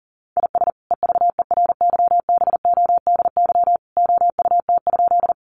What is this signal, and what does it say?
Morse code, heard as a keyed tone, about 40 wpm, 700 Hz. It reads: IS EVERYBODY OUT?
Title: Sentences from Top 1400 Words 40wpm